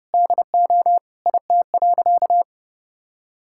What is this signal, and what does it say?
Morse code, heard as a keyed tone, about 32 wpm, 700 Hz. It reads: DO IT.
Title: Sentences from Top 1000 Words - Review Part 5 - 30wpm